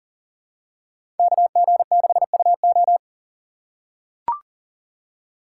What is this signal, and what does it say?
Morse code, heard as a keyed tone, about 40 wpm, 700 Hz. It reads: KG6UO
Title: USA Call Signs - 2x Character Spacing - 40wpm